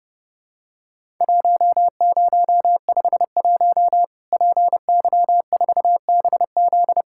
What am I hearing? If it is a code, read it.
Morse code, heard as a keyed tone, about 30 wpm, 700 Hz. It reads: 1051 PY4BZ
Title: Worked All Europe DX Contest - QTC Exchange - 30wpm